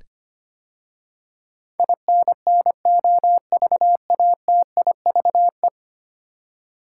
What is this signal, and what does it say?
Morse code, heard as a keyed tone, about 25 wpm, 700 Hz. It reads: INNOVATIVE
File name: 10 Letter Words - First 3 Letters Given - 25wpm